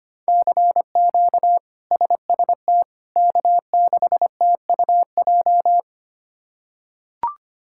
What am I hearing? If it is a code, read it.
Morse code, heard as a keyed tone, about 25 wpm, 700 Hz. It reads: CQ SST K6TUJ
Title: K1USN SST QSO Practice - 25wpm